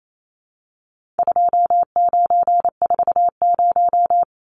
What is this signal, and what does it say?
Morse code, heard as a keyed tone, about 28 wpm, 700 Hz. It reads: 2940
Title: Numbers - 4-Digit 28wpm